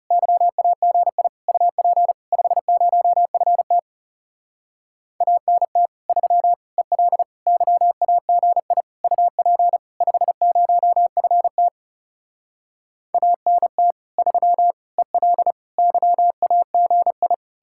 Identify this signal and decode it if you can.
Morse code, heard as a keyed tone, about 37 wpm, 700 Hz. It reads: YAGI UP 50FT ANT 3 EL YAGI UP 50FT ANT 3 EL YAGI
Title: Speed-Racing Common QSO Phrases - 30wpm